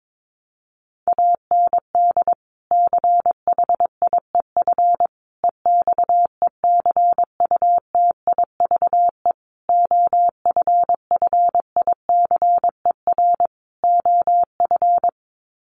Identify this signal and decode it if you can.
Morse code, heard as a keyed tone, about 22 wpm, 700 Hz. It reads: AND CHIEF EXECUTIVE OFFICER OF
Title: Sets of 5 Words Encore2 22wpm